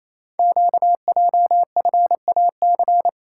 Morse code, heard as a keyed tone, about 27 wpm, 700 Hz. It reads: QJFAC
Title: Code Groups - 5 Letters - 28wpm